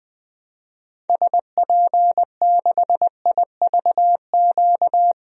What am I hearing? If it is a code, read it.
Morse code, heard as a keyed tone, about 20 wpm, 700 Hz. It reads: SP6IVQ
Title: DXCC-100 Call Signs - 6 Characters - Repeated 2x - 20wpm